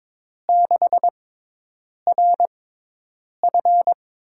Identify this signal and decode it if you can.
Morse code, heard as a keyed tone, about 22 wpm, spaced as for 8 wpm, 700 Hz. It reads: 6RF
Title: USA Call Signs - 5 Characters - 6x Character Spacing - 22wpm